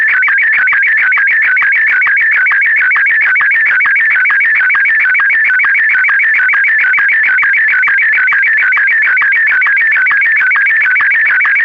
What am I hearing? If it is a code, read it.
SSTV (slow-scan television), Martin 1.